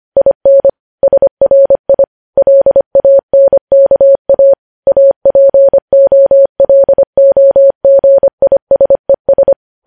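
Morse code, heard as a keyed tone, about 25 wpm, 550 Hz. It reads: IN SRI LANKA APOLOGISES
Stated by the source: Morse code news headlines sent at 25 wpm.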